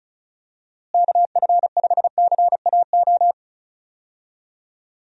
Morse code, heard as a keyed tone, about 34 wpm, 700 Hz. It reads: KF5CAO
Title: USA Call Signs - 6 Characters - Repeated 3x - 35wpm